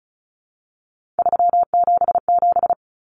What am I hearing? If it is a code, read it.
Morse code, heard as a keyed tone, about 34 wpm, 700 Hz. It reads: 377
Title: Numbers - 3-Digit 35wpm